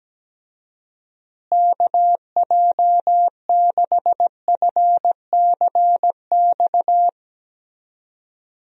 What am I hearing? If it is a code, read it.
Morse code, heard as a keyed tone, about 17 wpm, 700 Hz. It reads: KJ6FCX
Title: USA Call Signs - 6 Characters - Repeated 3x - 17wpm